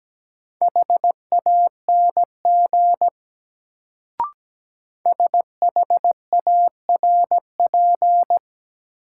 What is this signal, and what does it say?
Morse code, heard as a keyed tone, about 17 wpm, 700 Hz. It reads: HANG SHARP
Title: Sentences from Top 1600 Words - New Word Drill Alt - 17wpm